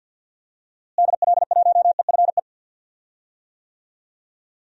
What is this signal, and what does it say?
Morse code, heard as a keyed tone, about 49 wpm, 700 Hz. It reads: DL1EFE
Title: DXCC-100 Call Signs - 6 Characters - Repeated 2x - 50wpm